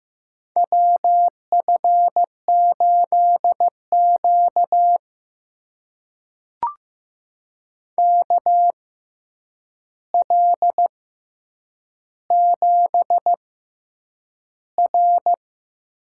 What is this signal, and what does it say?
Morse code, heard as a keyed tone, about 15 wpm, spaced as for 8 wpm, 700 Hz. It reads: WF8Q KL7R
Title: USA Call Signs - 4 Characters - 6x Character Spacing - 15wpm